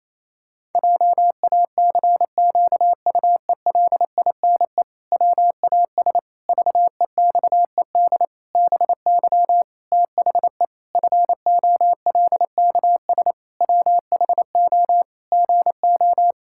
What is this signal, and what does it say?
Morse code, heard as a keyed tone, about 28 wpm, 700 Hz. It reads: JACQUELINE WAS VEXED BY THE FOLKS WHO GO
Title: Pangram Challenge v2 - 28wpm